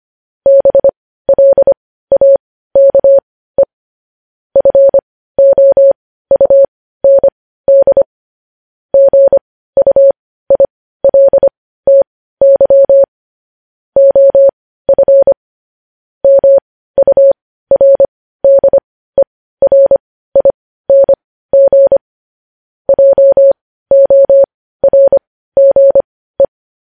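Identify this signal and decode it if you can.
Morse code, heard as a keyed tone, about 25 wpm, spaced as for 15 wpm, 550 Hz. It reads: BLAKE FOUND GUILTY OF MURDERING JORGE
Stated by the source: Morse code news headlines sent at 15 wpm.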